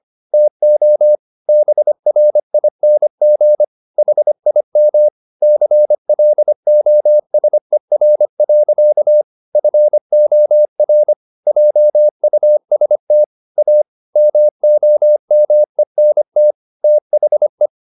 Morse code, heard as a keyed tone, about 25 wpm, 600 Hz. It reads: TO BRING HIM CLOSER. FOR JUST A MOMENT THE